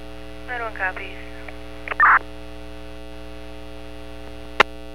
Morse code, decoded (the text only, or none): none